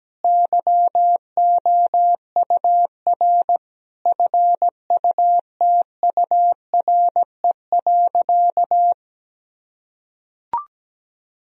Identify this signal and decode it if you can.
Morse code, heard as a keyed tone, about 17 wpm, 700 Hz. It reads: YOUR FUTURE.
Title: Sentences from Top 1000 Words - Review Part 4 - 17wpm